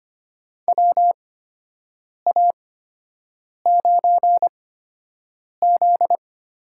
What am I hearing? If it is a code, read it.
Morse code, heard as a keyed tone, about 25 wpm, spaced as for 7 wpm, 700 Hz. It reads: WA9Z
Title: USA Call Signs - 6 Characters - 8x Character Spacing - 25wpm